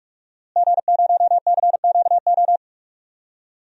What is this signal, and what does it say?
Morse code, heard as a keyed tone, about 46 wpm, 700 Hz. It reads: G0CQO
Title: DXCC-100 Call Signs - 5 Characters - Repeated 2x - 45wpm